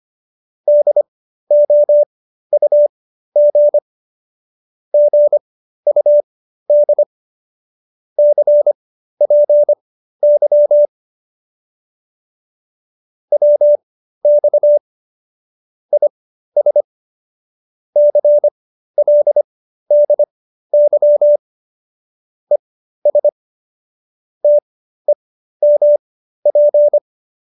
Morse code, heard as a keyed tone, about 25 wpm, spaced as for 13 wpm, 590 Hz. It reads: DOUG GUD CPY WX IS CLDY ES TEMP